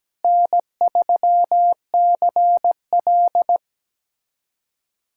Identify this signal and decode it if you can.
Morse code, heard as a keyed tone, about 17 wpm, 700 Hz. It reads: N3CL
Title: USA Callsign DE Callsign - 4 Characters - 17wpm